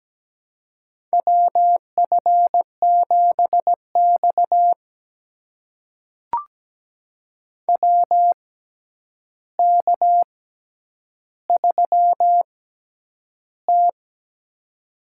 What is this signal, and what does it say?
Morse code, heard as a keyed tone, about 17 wpm, spaced as for 9 wpm, 700 Hz. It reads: WF7X WK3T
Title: USA Call Signs - 4 Characters - 6x Character Spacing - 17wpm